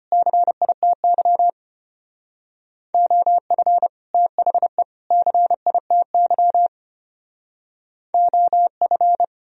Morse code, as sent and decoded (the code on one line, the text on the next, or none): -.-. .. - -.-- / --- ..-. / - .... / -.-. .. - -.-- / --- ..-.
CITY OF THE CITY OF